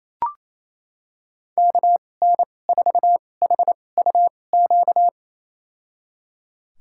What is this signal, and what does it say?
Morse code, heard as a keyed tone, about 28 wpm, spaced as for 20 wpm, 700 Hz. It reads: KN4HUQ
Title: USA Call Signs - 6 Characters - 2x Character Spacing - 28wpm